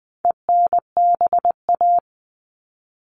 Morse code, heard as a keyed tone, about 20 wpm, 700 Hz. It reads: ENBA
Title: All Cut Numbers - 4-Digit 20wpm